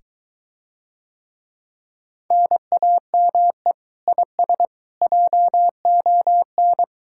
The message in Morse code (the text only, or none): NAME IS JON